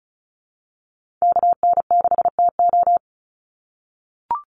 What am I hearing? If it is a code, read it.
Morse code, heard as a keyed tone, about 34 wpm, 700 Hz. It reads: KN6TO
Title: USA Call Signs 35wpm